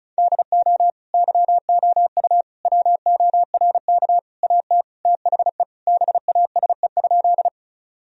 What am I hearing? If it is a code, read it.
Morse code, heard as a keyed tone, about 36 wpm, 700 Hz. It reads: DO YOU WORK AT THE BASE?
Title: Sentences from Top 1000 Words - Review Part 1 - 35wpm